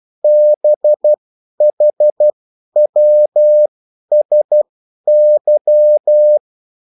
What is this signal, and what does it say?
Morse code, heard as a keyed tone, about 12 wpm, spaced as for 10 wpm, 600 Hz. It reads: BHWSY
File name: morse mp3 / revision9-vitesse 10 mots